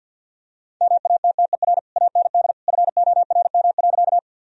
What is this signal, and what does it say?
Morse code, heard as a keyed tone, about 52 wpm, 700 Hz. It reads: MATTER AND FORM.
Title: Sentences from Top 1000 Words - Review Part 1 - 50wpm